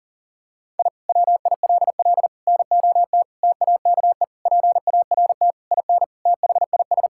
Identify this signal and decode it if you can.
Morse code, heard as a keyed tone, about 40 wpm, 700 Hz. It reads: I WILL NOT TAKE PART IN THIS
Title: Sentences from Top 1000 Words - Review Part 6 - 40wpm